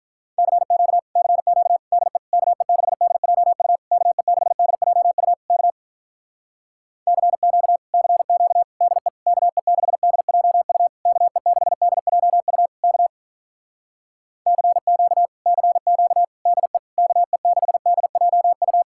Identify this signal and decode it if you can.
Morse code, heard as a keyed tone, about 48 wpm, 700 Hz. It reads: CQ CQ DE KE6DJU KE6DJU K CQ CQ DE KE6DJU KE6DJU K CQ CQ DE KE6DJU